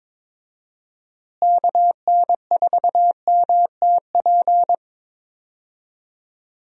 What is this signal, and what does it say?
Morse code, heard as a keyed tone, about 21 wpm, 700 Hz. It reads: KN4MTP
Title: USA Call Signs - 6 Characters - Repeated 3x - 22wpm